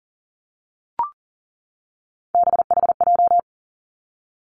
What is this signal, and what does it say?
Morse code, heard as a keyed tone, about 42 wpm, 700 Hz. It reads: BHJ
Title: Three Challenging Letter-Number Combinations 40wpm